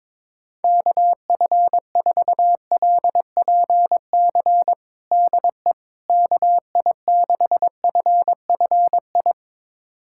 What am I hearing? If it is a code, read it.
Morse code, heard as a keyed tone, about 22 wpm, 700 Hz. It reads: KF4LPC DE KI6FFI
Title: USA Callsign DE Callsign - 6 Characters - 22wpm